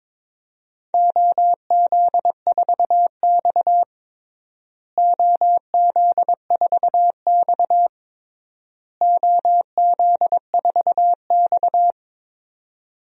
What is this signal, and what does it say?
Morse code, heard as a keyed tone, about 22 wpm, 700 Hz. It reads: OZ4X OZ4X OZ4X
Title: DXCC-100 Call Signs - 4 Characters - Repeated 3x - 22wpm